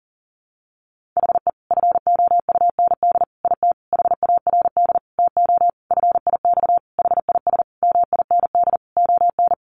Morse code, heard as a keyed tone, about 37 wpm, 700 Hz. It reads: HE FOUND IT HARD TO FIX HIS MIND ON
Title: Sentences from Top 1400 Words 40wpm